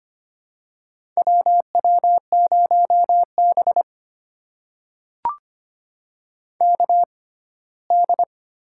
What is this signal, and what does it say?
Morse code, heard as a keyed tone, about 25 wpm, spaced as for 17 wpm, 700 Hz. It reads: WW0B KD
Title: USA Call Signs - 6x Character Spacing - 25wpm